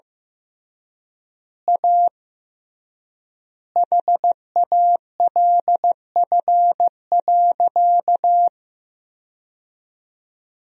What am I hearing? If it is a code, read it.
Morse code, heard as a keyed tone, about 15 wpm, 700 Hz. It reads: A HALF.
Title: 3x Word-Spacing - Sentences from Top 400 Words - 15wpm